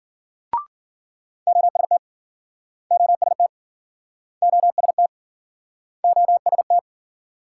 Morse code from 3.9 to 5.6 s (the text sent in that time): OST